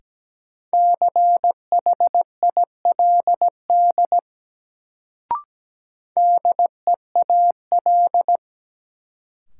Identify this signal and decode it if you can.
Morse code, heard as a keyed tone, about 17 wpm, 700 Hz. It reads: CHILD DEAL
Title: Sentences from Top 800 Words - New Word Drill - 17wpm